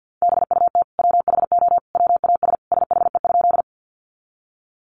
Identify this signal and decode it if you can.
Morse code, heard as a keyed tone, about 50 wpm, 700 Hz. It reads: BUT WHO WAS SHE?
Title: Sentences from Top 100 Words 50wpm